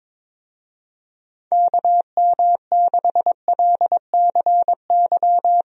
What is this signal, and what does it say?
Morse code, heard as a keyed tone, about 22 wpm, 700 Hz. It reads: KM6LCY
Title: USA Call Signs - 6 Characters - Repeated 2x - 22wpm